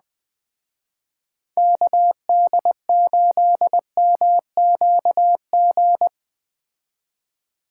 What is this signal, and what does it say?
Morse code, heard as a keyed tone, about 20 wpm, 700 Hz. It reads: KD8MQG